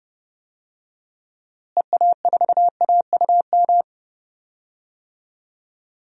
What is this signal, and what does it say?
Morse code, heard as a keyed tone, about 32 wpm, 700 Hz. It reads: EA4AUM
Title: DXCC-100 Call Signs - 6 Characters - Repeated 3x - 30wpm